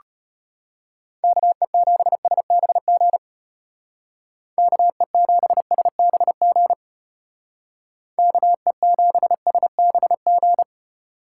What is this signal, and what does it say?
Morse code, heard as a keyed tone, about 33 wpm, 700 Hz. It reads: KE7SBG KE7SBG KE7SBG